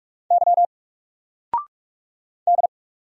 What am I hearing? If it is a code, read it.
Morse code, heard as a keyed tone, about 44 wpm, 700 Hz. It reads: Y D
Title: Single Character (Letter:Number:Punctuation:Prosign) 45wpm